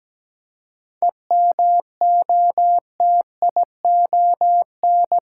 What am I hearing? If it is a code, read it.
Morse code, heard as a keyed tone, about 17 wpm, 700 Hz. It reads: EMOTION
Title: Seven Letter Words - 17wpm